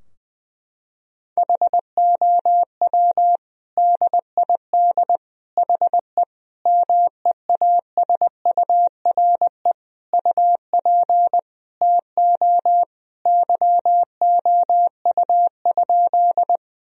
Morse code, heard as a keyed tone, about 20 wpm, 700 Hz. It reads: HOW DID HE MEASURE UP TO YOU?
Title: Sentences from Top 1000 Words - Review Part 2 - 20wpm